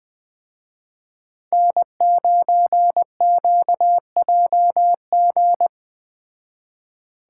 Morse code, heard as a keyed tone, about 20 wpm, 700 Hz. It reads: N9QJG
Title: Speed Racing - USA 1x3 Call Signs 17wpm